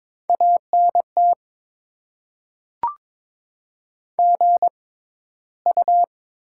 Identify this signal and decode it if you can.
Morse code, heard as a keyed tone, about 22 wpm, spaced as for 13 wpm, 700 Hz. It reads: ANT GU
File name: Top 20 QSO Elements - 6x Character Spacing - 22wpm